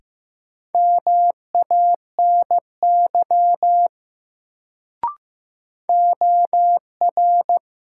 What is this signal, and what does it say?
Morse code, heard as a keyed tone, about 15 wpm, 700 Hz. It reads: MANY OR
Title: Sentences from Top 100 Words - New Word Drill - 15wpm